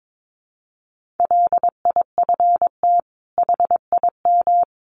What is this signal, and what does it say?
Morse code, heard as a keyed tone, about 22 wpm, 700 Hz. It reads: LIFT HIM
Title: Sentences from Top 1000 Words 22wpm